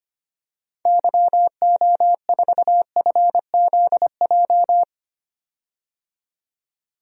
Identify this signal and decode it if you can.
Morse code, heard as a keyed tone, about 25 wpm, 700 Hz. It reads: YO4FZJ
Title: DXCC-100 Call Signs - 6 Characters - 25wpm